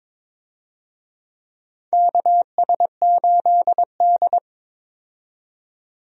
Morse code, heard as a keyed tone, about 22 wpm, 700 Hz. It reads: KS8D
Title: USA Call Signs - 4 Characters - 22wpm